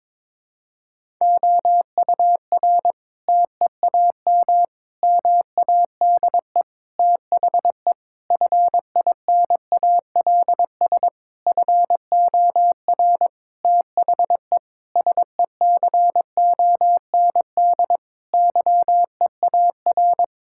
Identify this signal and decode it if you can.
Morse code, heard as a keyed tone, about 22 wpm, 700 Hz. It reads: OUR TEAM MADE THE FINALS FOR THE SECOND YEAR
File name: Sentences with 7-Word Phrases - 22wpm